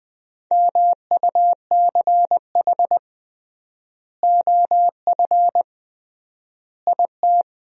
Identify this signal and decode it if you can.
Morse code, heard as a keyed tone, about 21 wpm, 700 Hz. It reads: MUCH OF IT
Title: 3x Word-Spacing - Sentences from Top 400 Words - 20wpm